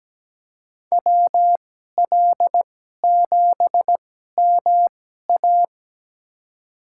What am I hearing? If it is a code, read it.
Morse code, heard as a keyed tone, about 17 wpm, spaced as for 12 wpm, 700 Hz. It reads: WL7MA
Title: USA Call Signs - 5 Characters - 2x Character Spacing - 17wpm